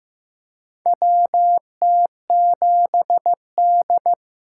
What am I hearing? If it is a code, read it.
Morse code, heard as a keyed tone, about 15 wpm, 700 Hz. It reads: WT7D
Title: USA Call Signs - 4 Characters - 3x Character Spacing - 15wpm